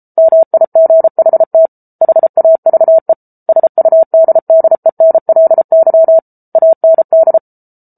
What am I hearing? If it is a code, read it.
Morse code, heard as a keyed tone, about 34 wpm, 650 Hz. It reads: MIGHT HAVE SUDDENLY AND